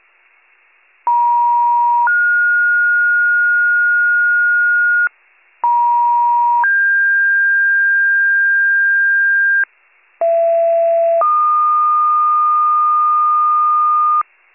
Two-tone page, heard as a one-second tone, then a three-second tone, about 950 then 1430 Hz, about 950 then 1600 Hz, about 670 then 1220 Hz, with noise. It is heard from 1.1 to 14.2 s.